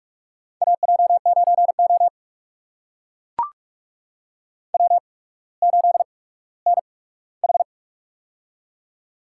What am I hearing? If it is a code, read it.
Morse code, heard as a keyed tone, about 44 wpm, spaced as for 19 wpm, 690 Hz. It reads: AJ9O W8NH
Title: USA Call Signs - 4 Characters - 8x Character Spacing - 45wpm